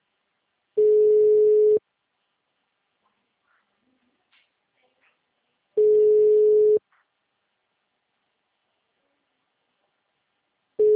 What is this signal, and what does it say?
Ringback tone (Europe, 425 Hz).